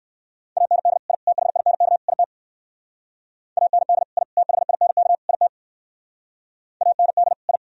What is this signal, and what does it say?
Morse code, heard as a keyed tone, about 67 wpm, 700 Hz. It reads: AND I THINK IT AND I THINK IT AND I